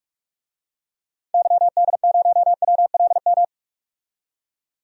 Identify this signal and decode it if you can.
Morse code, heard as a keyed tone, about 46 wpm, 700 Hz. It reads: YD0WLM
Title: DXCC-100 Call Signs - 6 Characters - 45wpm